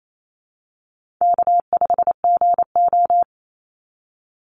Morse code, heard as a keyed tone, about 28 wpm, 700 Hz. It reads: K5GO
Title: Call Signs Repeated 2x - 28wpm